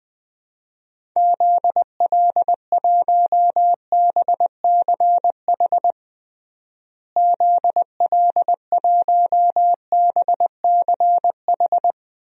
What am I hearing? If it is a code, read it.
Morse code, heard as a keyed tone, about 20 wpm, 700 Hz. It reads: ZL1BCH ZL1BCH
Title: DXCC-100 Call Signs - 6 Characters - Repeated 2x - 20wpm